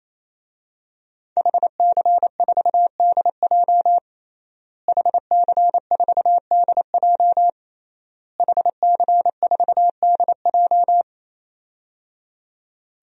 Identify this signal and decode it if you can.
Morse code, heard as a keyed tone, about 27 wpm, 700 Hz. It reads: HC4DJ HC4DJ HC4DJ
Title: DXCC-100 Call Signs - 5 Characters - Repeated 3x - 28wpm